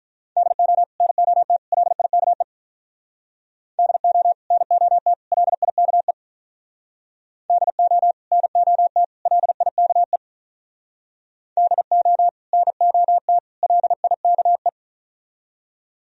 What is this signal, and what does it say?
Morse code, heard as a keyed tone, about 44 wpm, 700 Hz. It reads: DO NOT LIKE DO NOT LIKE DO NOT LIKE DO NOT LIKE